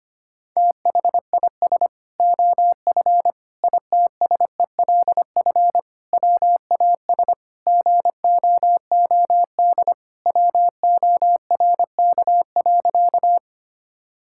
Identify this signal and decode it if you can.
Morse code, heard as a keyed tone, about 25 wpm, 700 Hz. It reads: THIS OF ITSELF WAS GOOD WORK.
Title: Sentences from Top 1000 Words - Review Part 5 - 25wpm